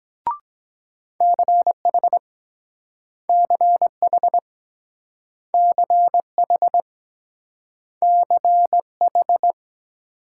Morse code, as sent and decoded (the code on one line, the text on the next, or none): -.-. .... / -.-. .... / -.-. .... / -.-. ....
CH CH CH CH